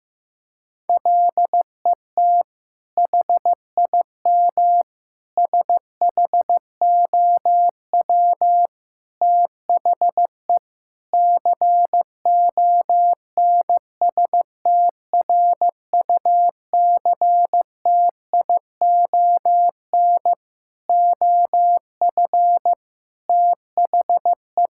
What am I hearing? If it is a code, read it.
Morse code, heard as a keyed tone, about 15 wpm, 700 Hz. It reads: LET HIM SHOW THE CONSTRUCTION OF THE
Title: Sentences from Top 1600 Words 15wpm